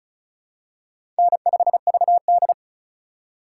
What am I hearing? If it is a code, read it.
Morse code, heard as a keyed tone, about 36 wpm, 700 Hz. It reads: N5VD
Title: USA Call Signs - 2x Character Spacing - 35wpm